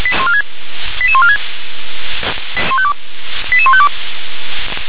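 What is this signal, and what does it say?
Five-tone selective call (ZVEI), heard as quick runs of five tones, tones of about 80 ms. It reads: 88146 88152
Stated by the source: Recorded with an SDR, ZVEI five-tone selective call.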